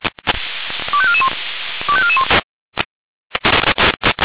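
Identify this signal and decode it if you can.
Five-tone selective call (ZVEI), heard as quick runs of five tones, tones of about 70 ms. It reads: 25441 25441
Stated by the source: ZVEI five-tone selective call received on SDR.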